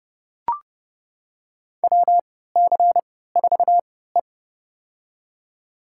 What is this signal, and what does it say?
Morse code, heard as a keyed tone, about 32 wpm, spaced as for 17 wpm, 700 Hz. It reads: WC4E
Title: USA Call Signs - 4 Characters - 3x Character Spacing - 30wpm